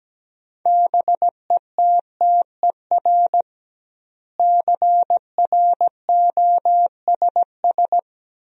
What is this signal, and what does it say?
Morse code, heard as a keyed tone, about 17 wpm, 700 Hz. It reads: BETTER CROSS
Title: 2x Word-Spacing - Sentences from Top 400 Words - 17wpm